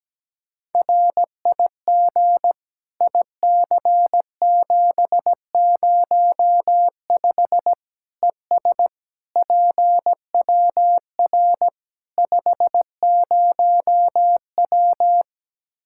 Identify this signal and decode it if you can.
Morse code, heard as a keyed tone, about 17 wpm, 700 Hz. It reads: RIG IC705 ES PWR 50W
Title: Practice QSO - Exchange 2 - Info - 17wpm